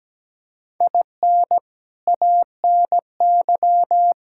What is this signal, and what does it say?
Morse code, heard as a keyed tone, about 17 wpm, 700 Hz. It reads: IN ANY
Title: Sentences from Top 1000 Words - Review Part 5 - 17wpm